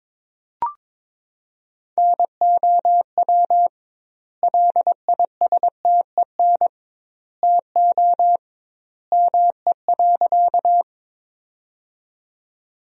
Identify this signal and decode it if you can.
Morse code, heard as a keyed tone, about 21 wpm, 700 Hz. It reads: NOW LISTEN TO ME.
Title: 2x Word-Spacing - Sentences from Top 400 Words - 22wpm